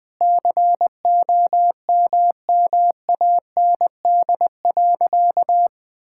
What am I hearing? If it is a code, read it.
Morse code, heard as a keyed tone, about 20 wpm, 700 Hz. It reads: COMMAND.
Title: Sentences from Top 1600 Words 20wpm